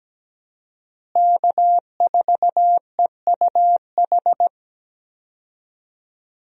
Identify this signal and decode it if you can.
Morse code, heard as a keyed tone, about 17 wpm, 700 Hz. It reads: K4EUH
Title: USA Call Signs - 5 Characters - Repeated 3x - 17wpm